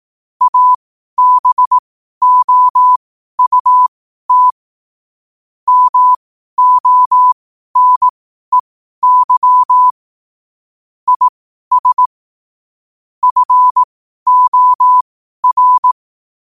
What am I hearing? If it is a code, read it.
Morse code, heard as a keyed tone, about 18 wpm, spaced as for 12 wpm, 1000 Hz. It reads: ABOUT MONEY IS FOR
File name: qotdfiles / 12wpm